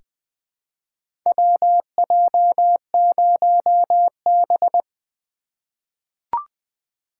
Morse code, heard as a keyed tone, about 20 wpm, 700 Hz. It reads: WJ0B
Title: USA Call Signs - 4x Character Spacing - 20wpm